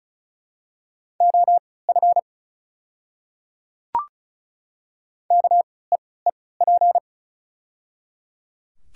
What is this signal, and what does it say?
Morse code, heard as a keyed tone, about 36 wpm, spaced as for 20 wpm, 690 Hz. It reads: OF KEEP
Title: 3x Character Spacing - Top 300 Words - 35wpm